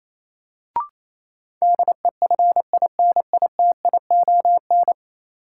Morse code, heard as a keyed tone, about 28 wpm, 700 Hz. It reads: DEFINITION
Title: Sentences from Top 1000 Words - Review Part 4 - New Word Drill - 28wpm